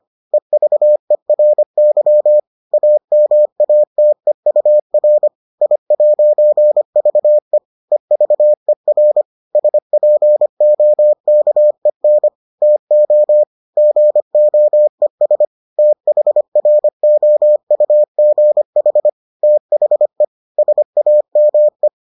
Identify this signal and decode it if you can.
Morse code, heard as a keyed tone, about 25 wpm, 600 Hz. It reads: EVERY AMATEUR I'VE EVER SPOKEN TO GOES THROUGH THE SAME